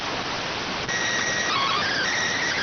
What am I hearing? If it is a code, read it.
SSTV (slow-scan television), PD 290.